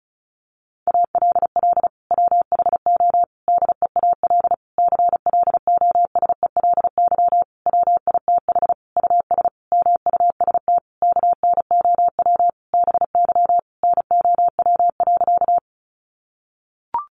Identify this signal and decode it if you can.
Morse code, heard as a keyed tone, about 34 wpm, 700 Hz. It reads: ALL WHO DEAL CLOSELY WITH US MUST KNOW BY NOW.
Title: Sentences from Top 1400 Words 35wpm